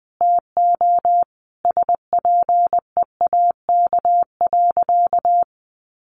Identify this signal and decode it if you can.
Morse code, heard as a keyed tone, about 20 wpm, 700 Hz. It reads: TO SPEAK.
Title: Sentences from Top 1100 Words 20wpm